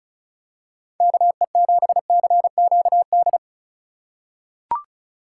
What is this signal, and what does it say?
Morse code, heard as a keyed tone, about 36 wpm, 700 Hz. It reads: KE7CQD
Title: USA Call Signs - 6 Characters - 4x Character Spacing - 35wpm